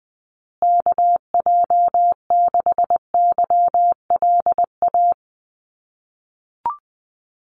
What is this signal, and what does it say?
Morse code, heard as a keyed tone, about 20 wpm, 700 Hz. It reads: KJ6YLA
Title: USA Call Signs Repeated 2x 20wpm